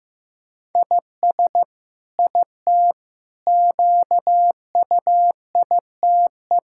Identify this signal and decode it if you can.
Morse code, heard as a keyed tone, about 14 wpm, 700 Hz. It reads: IS IT QUITE
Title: Sentences from Top 1000 Words - Review Part 6 - 15wpm